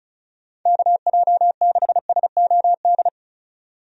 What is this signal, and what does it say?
Morse code, heard as a keyed tone, about 34 wpm, 700 Hz. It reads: KJ6SOD
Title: USA Call Signs - 6 Characters - 35wpm